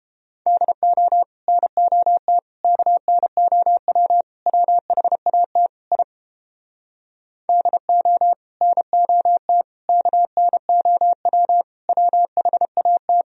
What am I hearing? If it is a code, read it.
Morse code, heard as a keyed tone, about 30 wpm, 700 Hz. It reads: DO NOT KNOW WHAT I DO NOT KNOW WHAT